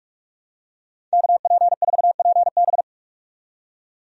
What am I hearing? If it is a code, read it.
Morse code, heard as a keyed tone, about 46 wpm, 700 Hz. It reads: KP4PB
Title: DXCC-100 Call Signs - 5 Characters - 45wpm